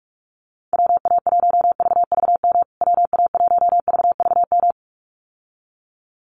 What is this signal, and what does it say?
Morse code, heard as a keyed tone, about 46 wpm, 700 Hz. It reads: WA1VVM WA1VVM
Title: USA Call Signs Repeated 2x 45wpm